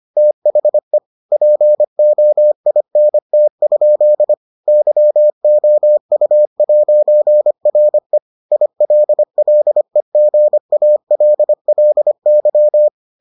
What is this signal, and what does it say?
Morse code, heard as a keyed tone, about 25 wpm, 600 Hz. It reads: THE POINT? YOU'RE ILLEGALLY